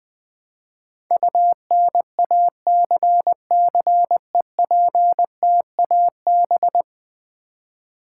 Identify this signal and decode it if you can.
Morse code, heard as a keyed tone, about 21 wpm, 700 Hz. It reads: UNACCEPTAB
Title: Build A Word - 5 Letters At A Time - 12-Letter Words - 20wpm